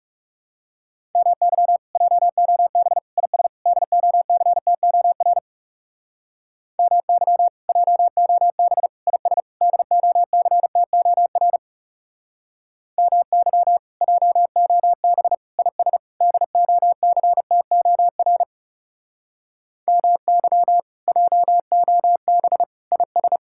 Morse code, heard as a keyed tone, about 36 wpm, 700 Hz. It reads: MY JOB IS DOCTOR MY JOB IS DOCTOR MY JOB IS DOCTOR MY JOB IS
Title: Speed-Racing Common QSO Phrases - 30wpm